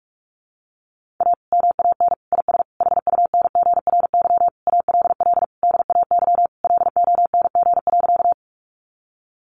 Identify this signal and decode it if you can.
Morse code, heard as a keyed tone, about 44 wpm, 700 Hz. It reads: A MAN IS HUNGRY ALL DAY LONG.
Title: Sentences from Top 1400 Words 45wpm